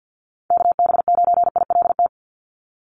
Morse code, heard as a keyed tone, about 49 wpm, 700 Hz. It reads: KB9ILT
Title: NA SOTA Chaser Callsigns 50wpm